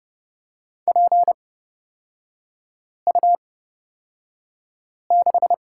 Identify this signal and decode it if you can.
Morse code, heard as a keyed tone, about 30 wpm, spaced as for 5 wpm, 700 Hz. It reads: PU6
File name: Lesson 36e - BT - Three Characters 30wpm